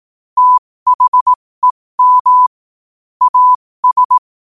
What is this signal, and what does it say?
Morse code, heard as a keyed tone, about 18 wpm, spaced as for 15 wpm, 1000 Hz. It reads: THEM AS
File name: qotdfiles / 15wpm